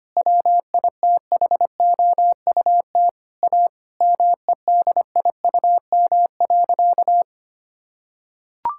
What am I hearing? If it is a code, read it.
Morse code, heard as a keyed tone, about 25 wpm, 700 Hz. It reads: WITHOUT A MEDIUM.
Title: Sentences from Top 1000 Words - Review Part 6 - 25wpm